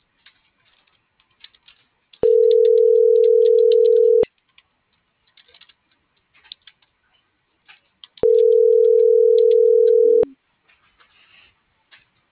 Ringback tone (North America), 440 + 480 Hz.